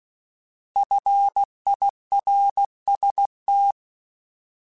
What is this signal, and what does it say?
Morse code, heard as a keyed tone, about 16 wpm, 800 Hz. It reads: FIRST
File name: Alphabetical / Alphabetical mp3s / Fs @15